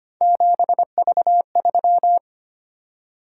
Morse code, heard as a keyed tone, about 29 wpm, 700 Hz. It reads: 7V3
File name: Speed-Racing - Three Letter-Number Combinations - 25wpm